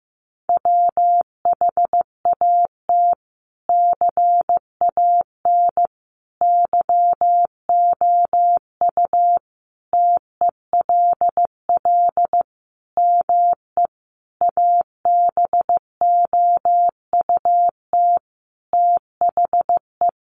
Morse code, heard as a keyed tone, about 15 wpm, 700 Hz. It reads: WHAT CAN YOU TELL ME ABOUT THE
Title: Sentences from Top 1500 Words 15wpm